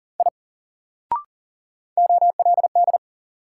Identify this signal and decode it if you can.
Morse code, heard as a keyed tone, about 40 wpm, 700 Hz. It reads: I OLD
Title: Sentences from Top 200 Words - New Word Drill - 40wpm